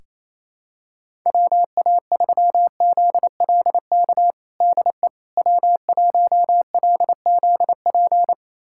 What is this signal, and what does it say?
Morse code, heard as a keyed tone, about 28 wpm, 700 Hz. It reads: WA3ZLK DE W1LZP
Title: USA Callsign DE Callsign 28wpm